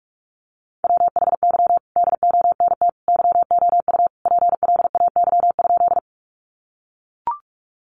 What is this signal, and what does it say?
Morse code, heard as a keyed tone, about 45 wpm, 700 Hz. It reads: WHY DONT YOU PLAY?
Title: Sentences from Top 300 Words 45wpm